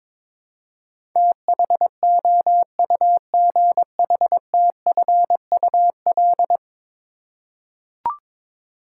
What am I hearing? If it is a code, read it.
Morse code, heard as a keyed tone, about 21 wpm, 700 Hz. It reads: THOUGHTFUL
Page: morse-code-ninja-practice-22wpm